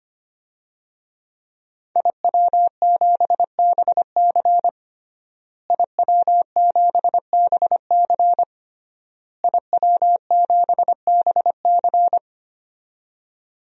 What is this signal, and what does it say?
Morse code, heard as a keyed tone, about 25 wpm, 700 Hz. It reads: IW7BC IW7BC IW7BC
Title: DXCC-100 Call Signs - 5 Characters - Repeated 3x - 25wpm